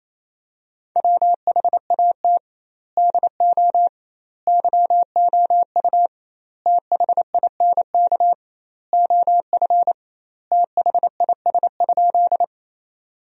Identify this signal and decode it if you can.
Morse code, heard as a keyed tone, about 27 wpm, 700 Hz. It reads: WHAT DO YOU THINK OF THIS?
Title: 2x Word-Spacing - Sentences from Top 200 Words - 28wpm